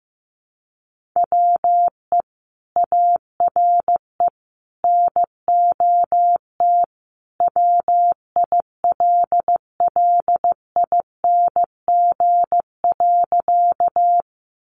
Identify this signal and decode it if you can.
Morse code, heard as a keyed tone, about 15 wpm, 700 Hz. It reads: WE ARE NOT WILLING.
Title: Sentences from Top 1000 Words 15wpm